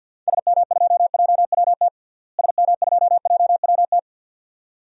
Morse code, heard as a keyed tone, about 49 wpm, 700 Hz. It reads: SM2JWT SM2JWT
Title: DXCC-100 Call Signs - 6 Characters - Repeated 2x - 50wpm